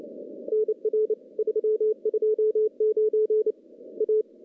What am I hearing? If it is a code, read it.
Morse code, heard as a keyed tone, about 27 wpm, 430 Hz. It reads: NR 329 A
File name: contests / CW-SS-k5tr- -1400